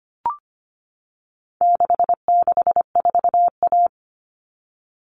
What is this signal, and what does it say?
Morse code, heard as a keyed tone, about 25 wpm, 700 Hz. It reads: 664A
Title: All Cut Numbers - 4-Digit 25wpm